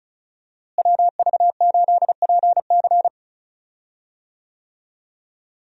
Morse code, heard as a keyed tone, about 36 wpm, 700 Hz. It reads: WV8PC
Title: USA Call Signs - 5 Characters - Repeated 3x - 35wpm